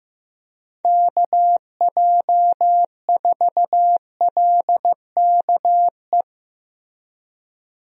Morse code, heard as a keyed tone, about 15 wpm, 700 Hz. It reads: KJ4LKE
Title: USA Call Signs - 6 Characters - Repeated 2x - 15wpm